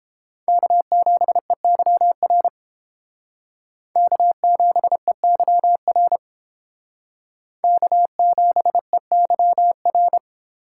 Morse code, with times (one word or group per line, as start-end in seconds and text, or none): 0.5-2.5 K7EYR
4.0-6.2 K7EYR
7.6-10.2 K7EYR